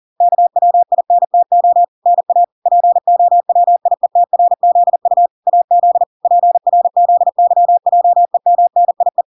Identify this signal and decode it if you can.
Morse code, heard as a keyed tone, about 40 wpm, 700 Hz. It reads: KWINTO NA POWIETRZU AZ PRZYJEMNIE